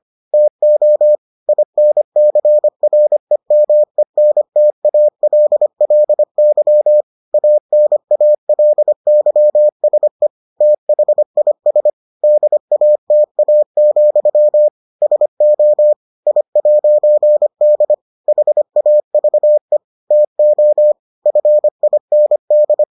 Morse code, heard as a keyed tone, about 25 wpm, 600 Hz. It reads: TO INCREMENTALLY ANALYSE THIS DATA, SO I'D HAVE TO FIND